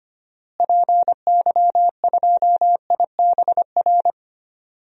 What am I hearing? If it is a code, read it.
Morse code, heard as a keyed tone, about 25 wpm, 700 Hz. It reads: PY2IBR
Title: DXCC-100 Call Signs - 6 Characters - Repeated 2x - 25wpm